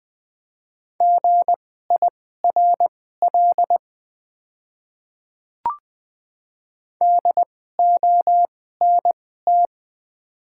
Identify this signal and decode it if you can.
Morse code, heard as a keyed tone, about 20 wpm, spaced as for 14 wpm, 700 Hz. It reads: GIRL DONT
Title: 2x Character Spacing - Top 500 Words - 20wpm